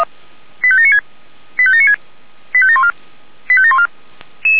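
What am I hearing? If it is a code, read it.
Five-tone selective call (ZVEI), heard as quick runs of five tones, tones of about 70 ms. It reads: 85796 85796 85714 85714
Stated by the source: ZVEI five-tone call received by SDR.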